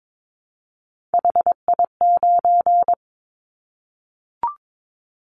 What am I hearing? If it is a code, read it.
Morse code, heard as a keyed tone, about 22 wpm, 700 Hz. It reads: HI9
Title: Callsign Prefixes for Top 100 DXCC - 22wpm